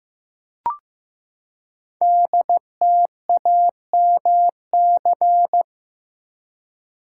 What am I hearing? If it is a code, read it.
Morse code, heard as a keyed tone, about 15 wpm, 700 Hz. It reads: DTAMC
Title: Code Groups - 5 Letters - 15wpm